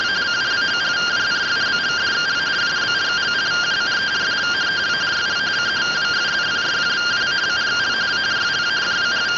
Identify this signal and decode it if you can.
Two-tone teleprinter signal, about 170 Hz shift.